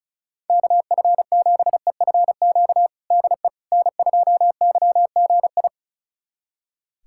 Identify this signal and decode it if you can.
Morse code, heard as a keyed tone, about 34 wpm, 700 Hz. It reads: KF7EFQ DE N2YGI
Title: USA Callsign DE Callsign 35wpm